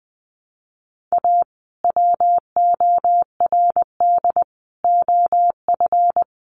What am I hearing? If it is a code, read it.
Morse code, heard as a keyed tone, about 20 wpm, 700 Hz. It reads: A WORD OF
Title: Sentences from Top 1100 Words 20wpm